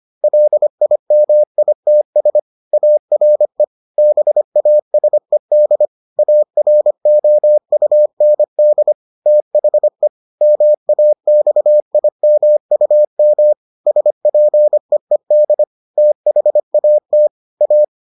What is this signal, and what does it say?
Morse code, heard as a keyed tone, about 25 wpm, 600 Hz. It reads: LIMITS ARE BASED AROUND THE MAXIMUM SPEED THAT A